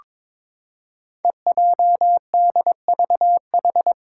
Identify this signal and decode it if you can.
Morse code, heard as a keyed tone, about 22 wpm, 700 Hz. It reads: EJDVH
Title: Five Challenging Letter-Number Combinations 22wpm